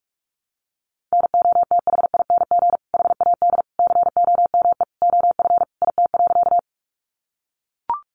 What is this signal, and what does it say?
Morse code, heard as a keyed tone, about 44 wpm, 700 Hz. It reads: NOTHING HAD COME OF IT.
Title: Sentences from Top 500 Words 45wpm